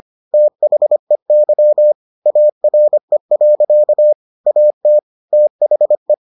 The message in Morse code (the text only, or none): THEY ARE. AT THE